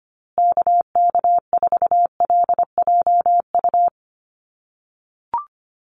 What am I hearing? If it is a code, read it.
Morse code, heard as a keyed tone, about 25 wpm, 700 Hz. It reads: KK4LJU
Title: USA Call Signs Repeated 2x 25wpm